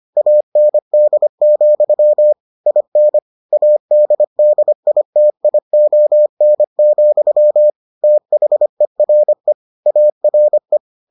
Morse code, heard as a keyed tone, about 25 wpm, 600 Hz. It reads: AND, IN ADDITION, THERE ARE